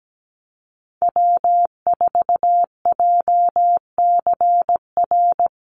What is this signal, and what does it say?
Morse code, heard as a keyed tone, about 17 wpm, 700 Hz. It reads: W4JCR
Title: USA Call Signs 17wpm